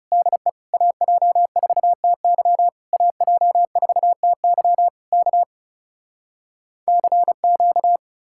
Morse code, heard as a keyed tone, about 40 wpm, 700 Hz. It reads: DE AJ4TY AJ4TY K CQ